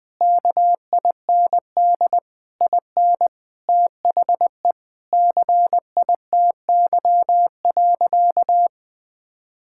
Morse code, heard as a keyed tone, about 20 wpm, 700 Hz. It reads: KIND IN THE CITY.
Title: Sentences with 6-Word Phrases - 20wpm